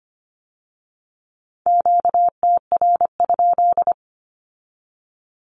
Morse code, heard as a keyed tone, about 25 wpm, 700 Hz. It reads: QTR?
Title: Q-Codes with Spoken Meaning - 25wpm